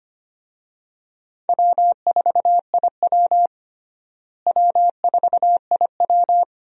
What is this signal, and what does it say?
Morse code, heard as a keyed tone, about 25 wpm, 700 Hz. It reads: W4IW W4IW
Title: USA Call Signs - 4 Characters - Repeated 2x - 25wpm